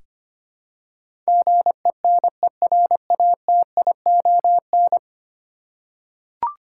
Morse code, heard as a keyed tone, about 25 wpm, 700 Hz. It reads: GENERATION
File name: Ten Letter Words - 25wpm